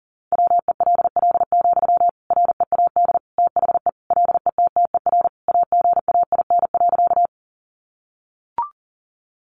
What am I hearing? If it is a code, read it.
Morse code, heard as a keyed tone, about 40 wpm, 700 Hz. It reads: WELL, READ THE LETTER AGAIN.
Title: Sentences from Top 300 Words 40wpm